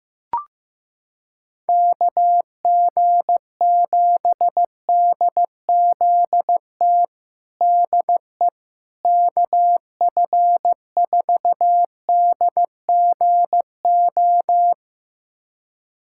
Morse code, heard as a keyed tone, about 15 wpm, 700 Hz. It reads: KG7DZT DE KF4DGO
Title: USA Callsign DE Callsign - 6 Characters - 15wpm